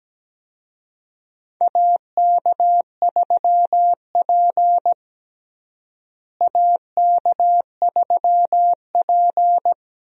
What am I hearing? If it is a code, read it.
Morse code, heard as a keyed tone, about 17 wpm, 700 Hz. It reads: AK3P AK3P
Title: USA Call Signs - 4 Characters - Repeated 2x - 17wpm